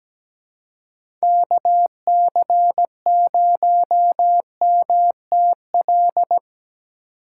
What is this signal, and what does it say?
Morse code, heard as a keyed tone, about 17 wpm, 700 Hz. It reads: KC0MTL
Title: USA Call Signs - 6 Characters - 3x Character Spacing - 17wpm